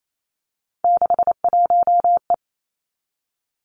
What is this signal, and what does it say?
Morse code, heard as a keyed tone, about 28 wpm, 700 Hz. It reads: 61E
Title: Three Challenging Letter-Number Combinations 28wpm